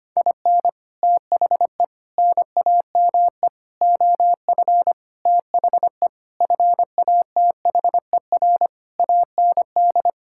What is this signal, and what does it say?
Morse code, heard as a keyed tone, about 25 wpm, 700 Hz. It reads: IN THE NAME OF THE FATHER AND
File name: practice / 25wpm / Sentences with 9-Word Phrases - 25wpm